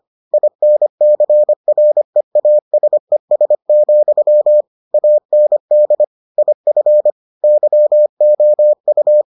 Morse code, heard as a keyed tone, about 25 wpm, 600 Hz. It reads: INCREASES, AND IF YOU